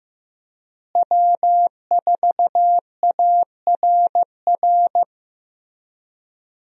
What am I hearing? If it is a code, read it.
Morse code, heard as a keyed tone, about 16 wpm, 700 Hz. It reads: W4ARR
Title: USA Call Signs - 5 Characters - Repeated 3x - 15wpm